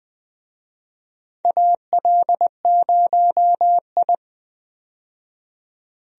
Morse code, heard as a keyed tone, about 20 wpm, 700 Hz. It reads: AL0I
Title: USA Call Signs - 4 Characters - 20wpm